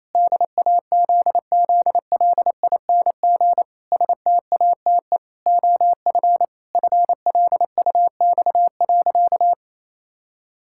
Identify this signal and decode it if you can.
Morse code, heard as a keyed tone, about 28 wpm, 700 Hz. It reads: DAZZLING STATE OF FLUX.
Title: Pangram Challenge v2 - 28wpm